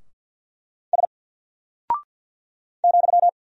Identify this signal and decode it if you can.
Morse code, heard as a keyed tone, about 49 wpm, 700 Hz. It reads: S ,
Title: Single Character (Letter:Number:Punctuation:Prosign) 50wpm